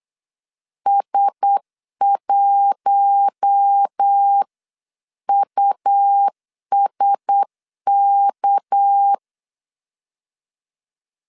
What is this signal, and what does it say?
Morse code, heard as a keyed tone, about 8 wpm, 800 Hz. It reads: S1USK
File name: CW / DET / 1 9 1500C S B 8wpm 5624